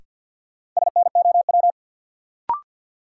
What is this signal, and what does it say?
Morse code, heard as a keyed tone, about 52 wpm, 700 Hz. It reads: SNOW
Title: Sentences from Top 500 Words - New Word Drill - 50wpm